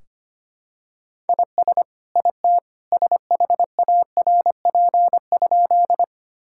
Morse code, heard as a keyed tone, about 25 wpm, 700 Hz. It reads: IS IT SHARP?
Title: Sentences from Top 1600 Words 25wpm